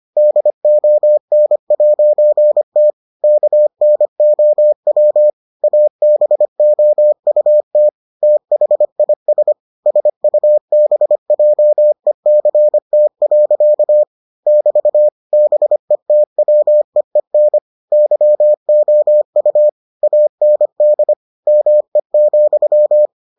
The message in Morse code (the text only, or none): DON'T KNOW ABOUT THIS SUBJECT. <BT> BETWEEN YOU AND ME,